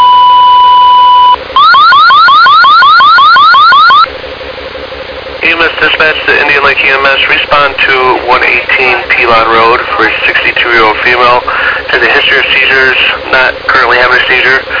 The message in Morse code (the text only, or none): none